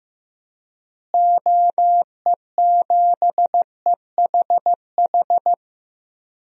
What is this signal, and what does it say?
Morse code, heard as a keyed tone, about 15 wpm, 700 Hz. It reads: OE7EHH
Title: DXCC-100 Call Signs - 6 Characters - Repeated 3x - 15wpm